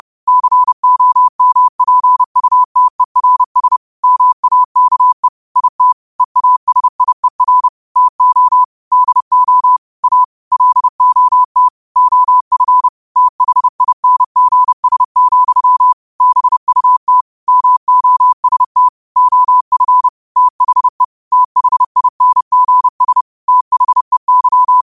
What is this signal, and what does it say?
Morse code, heard as a keyed tone, about 30 wpm, 1000 Hz. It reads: COMPUTERS MAKE IT EASIER TO DO A LOT OF THINGS, BUT MOST OF THE THINGS THEY
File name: qotdfiles / 30wpm